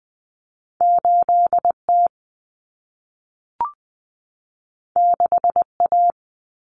Morse code, heard as a keyed tone, about 20 wpm, 700 Hz. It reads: 8T 6A
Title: Common Cut Numbers - 2-Digit 20wpm